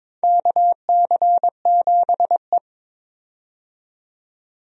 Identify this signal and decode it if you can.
Morse code, heard as a keyed tone, about 22 wpm, 700 Hz. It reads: KC7E
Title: USA Call Signs - 4 Characters - 22wpm